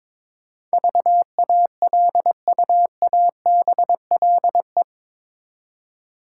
Morse code, heard as a keyed tone, about 22 wpm, 700 Hz. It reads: VALUABLE
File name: practice / 22wpm / Eight Letter Words - 22wpm